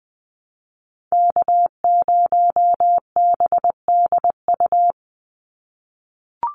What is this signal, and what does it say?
Morse code, heard as a keyed tone, about 20 wpm, 700 Hz. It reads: K0BDU
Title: USA Call Signs 20wpm